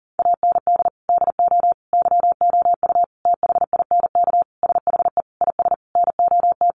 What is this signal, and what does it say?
Morse code, heard as a keyed tone, about 40 wpm, 700 Hz. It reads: AND DO YOU THINK SHE IS NOT
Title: Sentences from Top 600 Words 40wpm